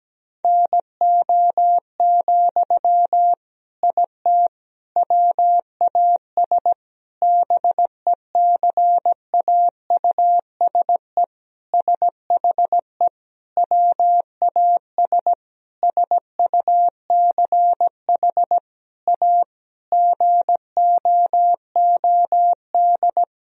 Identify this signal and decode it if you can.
Morse code, heard as a keyed tone, about 17 wpm, 700 Hz. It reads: NO, IT WAS BECAUSE SHE WAS SUCH A GOOD
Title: Sentences from Top 1000 Words - Review Part 4 - 17wpm